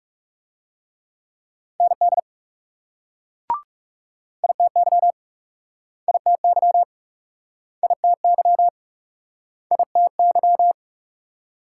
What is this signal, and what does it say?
Morse code, heard as a keyed tone, about 38 wpm, 700 Hz. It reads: ND ITY ITY ITY ITY